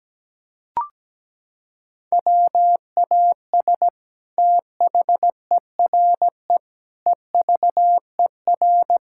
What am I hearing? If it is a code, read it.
Morse code, heard as a keyed tone, about 17 wpm, 700 Hz. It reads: WAS THERE EVER
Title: Sentences from Top 1000 Words - Review Part 6 - 17wpm